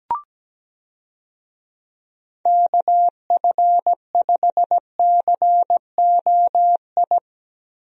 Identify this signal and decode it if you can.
Morse code, heard as a keyed tone, about 17 wpm, 700 Hz. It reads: KF5COI
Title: USA Call Signs - 6 Characters - 17wpm